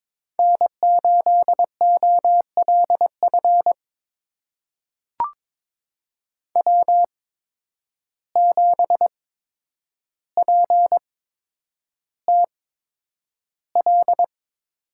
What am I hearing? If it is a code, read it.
Morse code, heard as a keyed tone, about 22 wpm, spaced as for 9 wpm, 700 Hz. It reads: N8OLF W7PTL
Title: USA Call Signs - 5 Characters - 8x Character Spacing - 22wpm